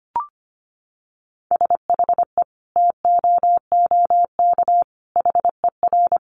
Morse code, heard as a keyed tone, about 25 wpm, 700 Hz. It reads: SHE TOOK HER
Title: Sentences from Top 900 Words 25wpm